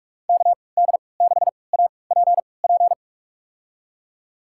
Morse code, heard as a keyed tone, about 46 wpm, spaced as for 25 wpm, 700 Hz. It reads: KD6APP
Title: USA Call Signs - 6 Characters - 3x Character Spacing - 45wpm